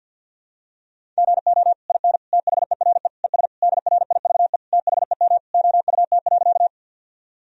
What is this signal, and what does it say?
Morse code, heard as a keyed tone, about 52 wpm, 700 Hz. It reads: GO IN THERE ES DRIVE THEM OUT.
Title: Sentences from Top 500 Words and Abbreviations 50wpm